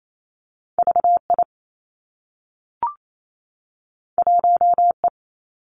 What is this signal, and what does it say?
Morse code, heard as a keyed tone, about 28 wpm, 700 Hz. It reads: V I 1 E
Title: Two Challenging Letter-Number Combinations 28wpm